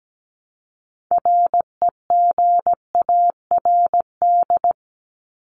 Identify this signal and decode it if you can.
Morse code, heard as a keyed tone, about 17 wpm, 700 Hz. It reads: REGARD
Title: ED Suffix - Top 100 Words with Base Words 17wpm